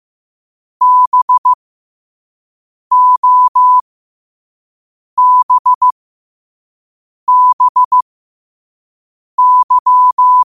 Morse code, heard as a keyed tone, about 15 wpm, spaced as for 5 wpm, 1000 Hz. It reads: BOBBY